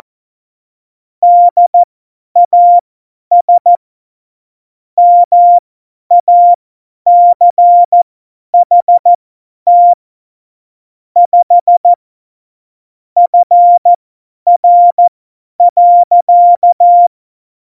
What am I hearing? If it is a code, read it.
Morse code, heard as a keyed tone, about 14 wpm, spaced as for 10 wpm, 700 Hz. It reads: DAS MACHT 5 FR.
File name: Morsekurs / Witze / Witz 05. Februar 2019, 50 F70